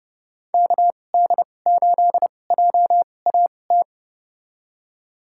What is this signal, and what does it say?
Morse code, heard as a keyed tone, about 32 wpm, spaced as for 22 wpm, 700 Hz. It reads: KD8JAT
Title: USA Call Signs - 6 Characters - 2x Character Spacing - 30wpm